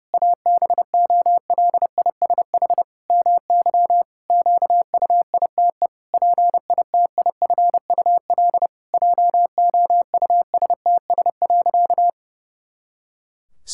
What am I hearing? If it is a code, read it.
Morse code, heard as a keyed tone, about 30 wpm, 700 Hz. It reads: ABOLISH MY QUITE PITIFUL JOUSTS.
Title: Pangram Challenge v2 - 30wpm